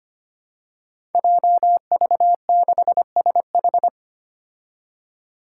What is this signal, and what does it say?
Morse code, heard as a keyed tone, about 25 wpm, 700 Hz. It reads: JV6SH
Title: Five Challenging Letter-Number Combinations 25wpm